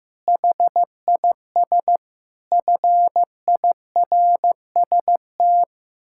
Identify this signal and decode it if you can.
Morse code, heard as a keyed tone, about 15 wpm, 700 Hz. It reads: HIS FIRST
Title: Pangram Challenge v2 - 15wpm